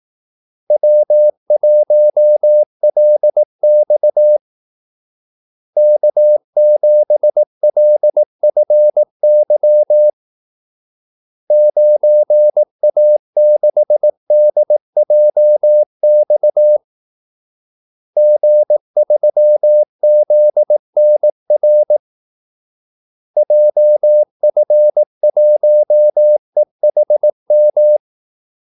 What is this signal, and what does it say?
Morse code, heard as a keyed tone, about 18 wpm, 600 Hz. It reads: W1LX K7LFY 9A6DJX G3ZNR JF1EHM
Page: Kaldesignaler 18 wpm